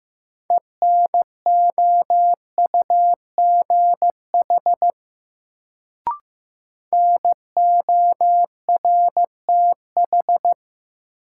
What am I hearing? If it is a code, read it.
Morse code, heard as a keyed tone, about 15 wpm, 700 Hz. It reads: ENOUGH NORTH
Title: Sentences from Top 400 Words - New Word Drill - 15wpm